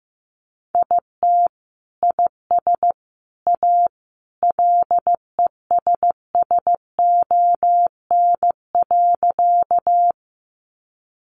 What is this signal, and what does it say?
Morse code, heard as a keyed tone, about 15 wpm, 700 Hz. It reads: IT IS A LESSON.
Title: Sentences from Top 1200 Words 15wpm